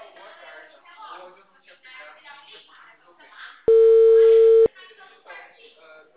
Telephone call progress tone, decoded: ringback tone

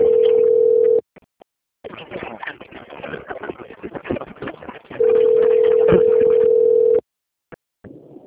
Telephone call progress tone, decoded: ringback tone

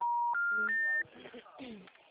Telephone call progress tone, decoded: special information tone

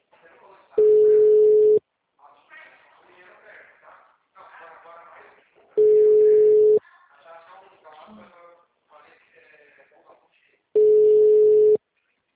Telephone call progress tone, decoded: ringback tone

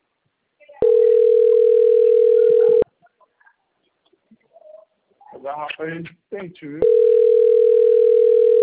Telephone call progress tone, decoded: ringback tone

